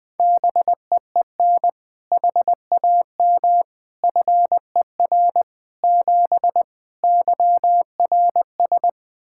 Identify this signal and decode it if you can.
Morse code, heard as a keyed tone, about 20 wpm, 700 Hz. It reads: BEEN HAM FER 7 YRS
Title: K1USN SST QSO Practice 20wpm